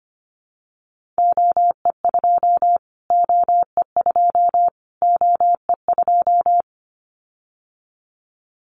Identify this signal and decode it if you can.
Morse code, heard as a keyed tone, about 25 wpm, 700 Hz. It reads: OE2 OE2 OE2
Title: Callsign Prefixes for Top 25 DXCC - Repeated 3x - 25wpm